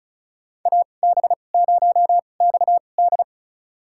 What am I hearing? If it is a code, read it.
Morse code, heard as a keyed tone, about 36 wpm, spaced as for 25 wpm, 700 Hz. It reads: AB0XD
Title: USA Call Signs - 2x Character Spacing - 35wpm